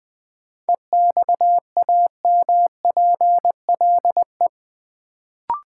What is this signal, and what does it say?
Morse code, heard as a keyed tone, about 20 wpm, 700 Hz. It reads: EXAMPLE
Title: Sets of 2 Words - New Word Drill - 20wpm